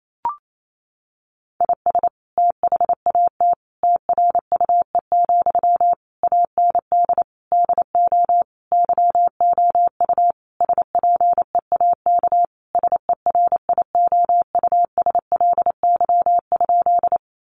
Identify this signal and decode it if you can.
Morse code, heard as a keyed tone, about 28 wpm, 700 Hz. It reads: IS THAT TRUE, AND DO YOU SPEAK SERIOUSLY?
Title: Sentences from Top 1200 Words 28wpm